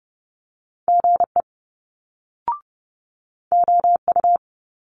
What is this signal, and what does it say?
Morse code, heard as a keyed tone, about 30 wpm, 700 Hz. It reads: GE OU
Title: Multi-letter Phonemes Quick Fire 30wpm